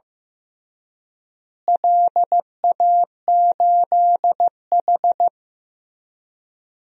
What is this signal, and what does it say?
Morse code, heard as a keyed tone, about 16 wpm, 700 Hz. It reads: LA8H
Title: DXCC-100 Call Signs - 4 Characters - Repeated 3x - 15wpm